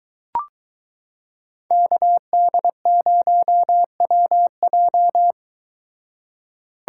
Morse code, heard as a keyed tone, about 22 wpm, 700 Hz. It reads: KD0WJ